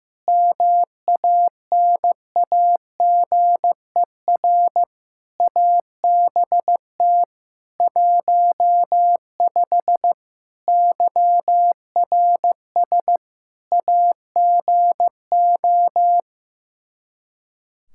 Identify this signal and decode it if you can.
Morse code, heard as a keyed tone, about 15 wpm, 700 Hz. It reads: MANAGER ABT 15 YRS AGO
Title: Practice QSO - Exchange 3 - More Info - 15wpm